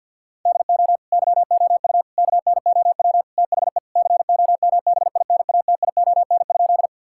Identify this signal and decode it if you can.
Morse code, heard as a keyed tone, about 50 wpm, 700 Hz. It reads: DO YOU KNOW THE COMBINATION?
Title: Sentences from Top 1000 Words - Review Part 6 - 50wpm